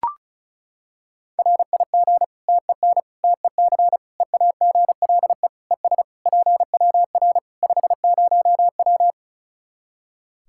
Morse code, heard as a keyed tone, about 34 wpm, 700 Hz. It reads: RIG TEN TEC EAGLE ES PWR 50W
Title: Practice QSO - Exchange 2 - Info - 35wpm